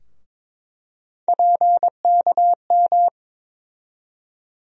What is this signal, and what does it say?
Morse code, heard as a keyed tone, about 22 wpm, 700 Hz. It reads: PKM
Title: USA Call Sign Postfixes - 3 Letters 22wpm